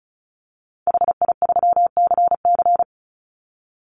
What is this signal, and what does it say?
Morse code, heard as a keyed tone, about 34 wpm, 700 Hz. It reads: HI3CC
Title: Call Signs - 35wpm